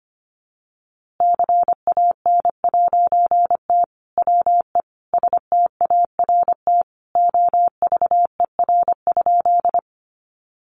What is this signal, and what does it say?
Morse code, heard as a keyed tone, about 25 wpm, 700 Hz. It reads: CAN'T WE START OVER?
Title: A Sound Of Thunder 25wpm